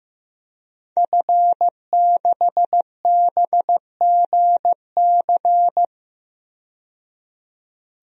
Morse code, heard as a keyed tone, about 15 wpm, 700 Hz. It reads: F6BGC
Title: DXCC-100 Call Signs - 5 Characters - 15wpm